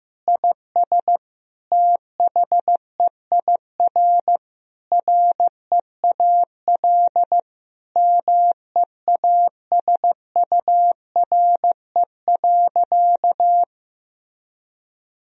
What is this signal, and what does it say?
Morse code, heard as a keyed tone, about 15 wpm, 700 Hz. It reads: IS THEIR REAL MEASURE.
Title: Sentences from Top 1000 Words - Review Part 1 - 15wpm